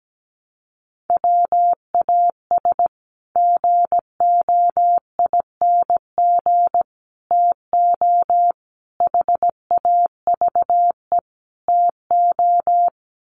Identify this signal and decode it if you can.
Morse code, heard as a keyed tone, about 17 wpm, 700 Hz. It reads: WAS GOING TO HAVE TO
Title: Sets of 5 Words Encore 17wpm